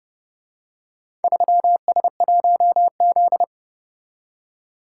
Morse code, heard as a keyed tone, about 29 wpm, 700 Hz. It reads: 3S1Z